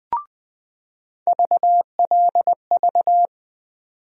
Morse code, heard as a keyed tone, about 20 wpm, 700 Hz. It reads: VLV
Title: USA Call Sign Postfixes - 3 Letters 20wpm